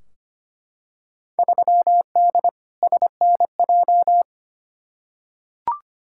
Morse code, heard as a keyed tone, about 25 wpm, 700 Hz. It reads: 3D SNJ
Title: Field Day Exchanges Encore - 25wpm